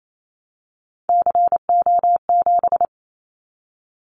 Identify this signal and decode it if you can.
Morse code, heard as a keyed tone, about 28 wpm, 700 Hz. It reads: CO7
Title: Callsign Prefixes for Top 100 DXCC - 28wpm